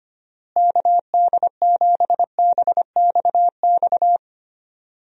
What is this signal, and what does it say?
Morse code, heard as a keyed tone, about 25 wpm, 700 Hz. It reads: KD7BXX
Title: USA Call Signs - 6 Characters - Repeated 3x - 25wpm